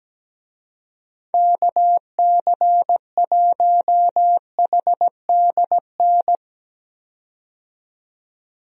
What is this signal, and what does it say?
Morse code, heard as a keyed tone, about 17 wpm, 700 Hz. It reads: KC1HDN
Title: USA Call Signs - 6 Characters - 17wpm